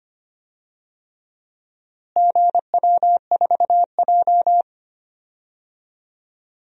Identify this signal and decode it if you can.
Morse code, heard as a keyed tone, about 25 wpm, 700 Hz. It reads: GW4J
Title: DXCC-100 Call Signs - 4 Characters - Repeated 3x - 25wpm